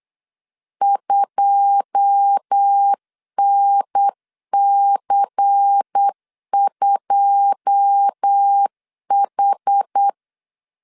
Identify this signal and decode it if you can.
Morse code, heard as a keyed tone, about 8 wpm, 800 Hz. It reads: 2NC2H
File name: LEZIONI / 9 2 8wpm 3437